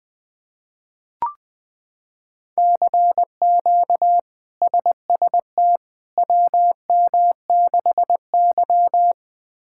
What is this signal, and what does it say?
Morse code, heard as a keyed tone, about 21 wpm, 700 Hz. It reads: CQ SST WM6Y
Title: K1USN SST QSO Practice - 20wpm